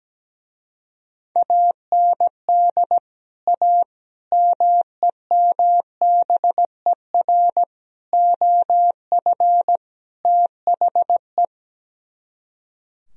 Morse code, heard as a keyed tone, about 17 wpm, 700 Hz. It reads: AND A MEMBER OF THE
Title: Speed-Racing - Sets of 5 Words 17wpm